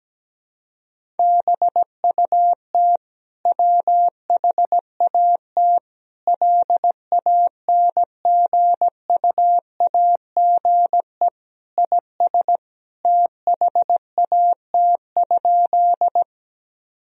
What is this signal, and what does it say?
Morse code, heard as a keyed tone, about 17 wpm, 700 Hz. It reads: BUT WHAT LANGUAGE IS THAT?
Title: Sentences from Top 1000 Words - Review Part 3 - 17wpm